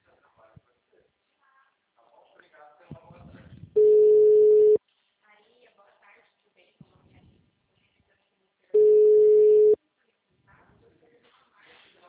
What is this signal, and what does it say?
Ringback tone (Europe, 425 Hz), with speech.
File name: out- -2011- - - .41350